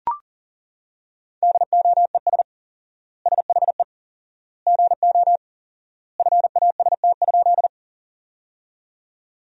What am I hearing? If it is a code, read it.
Morse code, heard as a keyed tone, about 39 wpm, 700 Hz. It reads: DOES SHE GO FAST?
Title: 4x Word-Spacing - Sentences from Top 400 Words - 40wpm